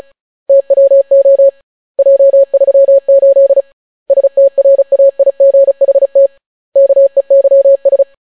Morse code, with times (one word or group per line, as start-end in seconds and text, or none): 0.5-1.5 TWO
2.0-3.6 J38
4.1-6.3 STRAIGHT
6.8-8.0 KEYS